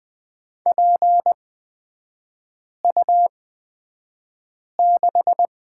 Morse code, heard as a keyed tone, about 20 wpm, spaced as for 5 wpm, 700 Hz. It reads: PU6
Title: 224 Lesson 36e - BT - Three Characters 20wpm